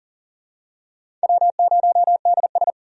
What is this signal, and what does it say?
Morse code, heard as a keyed tone, about 40 wpm, 690 Hz. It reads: W0DS
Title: USA Call Signs - 4 Characters - 40wpm